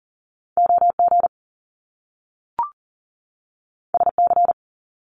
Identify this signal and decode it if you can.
Morse code, heard as a keyed tone, about 43 wpm, 700 Hz. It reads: OG SC
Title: Top Two Letter Combinations in Words 40wpm